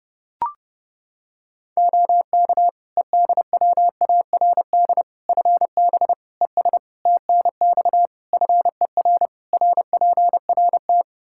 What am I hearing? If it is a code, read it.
Morse code, heard as a keyed tone, about 30 wpm, 700 Hz. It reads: OK EDWARD FB ES TNX FER RPRT
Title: Practice QSO - Exchange 2 - Info - 30wpm